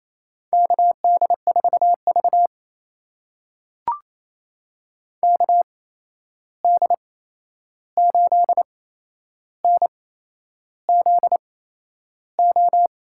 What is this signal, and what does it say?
Morse code, heard as a keyed tone, about 28 wpm, spaced as for 11 wpm, 700 Hz. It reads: KD4V KD8NZO
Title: USA Call Signs - 8x Character Spacing - 28wpm